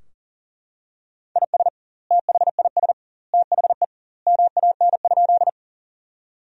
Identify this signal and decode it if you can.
Morse code, heard as a keyed tone, about 37 wpm, 700 Hz. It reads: IS THIS THE MAN?
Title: 2x Word-Spacing - Sentences from Top 200 Words - 40wpm